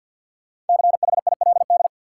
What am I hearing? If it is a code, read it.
Morse code, heard as a keyed tone, about 52 wpm, 700 Hz. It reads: CHILD
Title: Sentences from Top 800 Words - New Word Drill - 50wpm